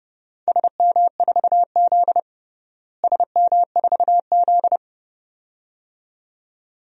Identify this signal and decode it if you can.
Morse code, heard as a keyed tone, about 30 wpm, 700 Hz. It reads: SM4Z SM4Z
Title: DXCC-100 Call Signs - 4 Characters - Repeated 2x - 30wpm